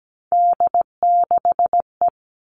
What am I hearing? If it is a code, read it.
Morse code, heard as a keyed tone, about 17 wpm, 700 Hz. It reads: D6E
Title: All Cut Numbers - 3-Digit 17wpm